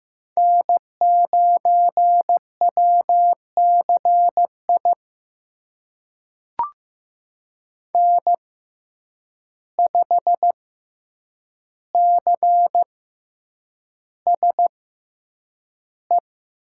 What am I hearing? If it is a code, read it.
Morse code, heard as a keyed tone, about 15 wpm, spaced as for 8 wpm, 700 Hz. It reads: N9WCI N5CSE
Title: USA Call Signs - 5 Characters - 6x Character Spacing - 15wpm